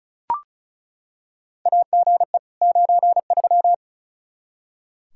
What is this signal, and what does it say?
Morse code, heard as a keyed tone, about 34 wpm, 700 Hz. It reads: AGE 93
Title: K1USN SST QSO Practice 35wpm